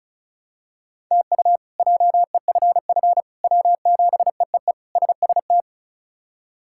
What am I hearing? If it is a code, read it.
Morse code, heard as a keyed tone, about 36 wpm, 700 Hz. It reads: TU JEFF W7EEE SST
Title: K1USN SST QSO Practice - 35wpm